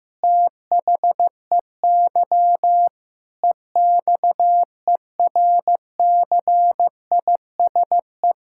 Morse code, heard as a keyed tone, about 15 wpm, 700 Hz. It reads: THEY EXERCISE
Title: Sentences from Top 1000 Words - Review Part 6 - 15wpm